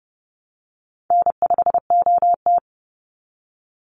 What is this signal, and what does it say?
Morse code, heard as a keyed tone, about 30 wpm, 700 Hz. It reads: N5OT
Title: Call Signs Repeated 3x - 30wpm